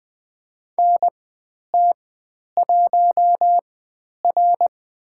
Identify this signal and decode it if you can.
Morse code, heard as a keyed tone, about 20 wpm, spaced as for 10 wpm, 700 Hz. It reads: NT1R
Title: 095 Lesson 16g - Letter c - Call Signs 20wpm